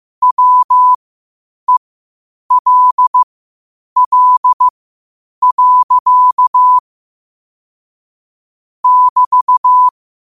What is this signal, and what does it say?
Morse code, heard as a keyed tone, about 15 wpm, spaced as for 8 wpm, 1000 Hz. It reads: WELL. <BT>
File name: 8wpm